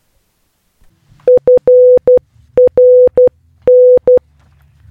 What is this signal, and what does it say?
Morse code, heard as a keyed tone, about 12 wpm, 500 Hz. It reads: FRN